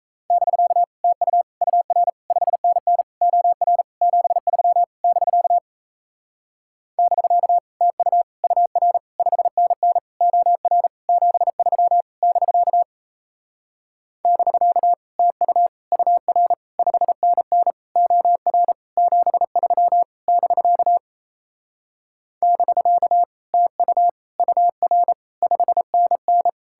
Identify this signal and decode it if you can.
Morse code, heard as a keyed tone, about 37 wpm, 700 Hz. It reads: <BK> TU UR 5NN OR 73 <BK> <BK> TU UR 5NN OR 73 <BK> <BK> TU UR 5NN OR 73 <BK> <BK> TU UR 5NN